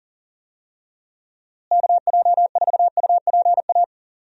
Morse code, heard as a keyed tone, about 39 wpm, 700 Hz. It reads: KJ4UPA
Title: USA Call Signs - 6 Characters - Repeated 2x - 40wpm